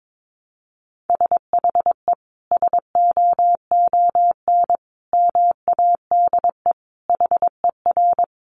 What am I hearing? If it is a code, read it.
Morse code, heard as a keyed tone, about 22 wpm, 700 Hz. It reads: SHE SOON MADE HER
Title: Sentences from Top 1100 Words 22wpm